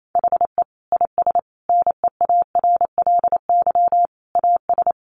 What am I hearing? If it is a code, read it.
Morse code, heard as a keyed tone, about 28 wpm, 700 Hz. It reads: HE IS NEARLY AS
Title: Sentences from Top 1300 Words 28wpm